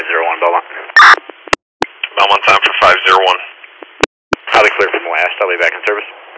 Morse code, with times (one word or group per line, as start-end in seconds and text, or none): none